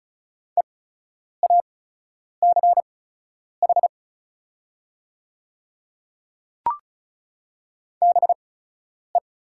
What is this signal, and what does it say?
Morse code, heard as a keyed tone, about 36 wpm, spaced as for 10 wpm, 690 Hz. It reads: EACH BE
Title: 8x Character Spacing - Top 100 Words - 35wpm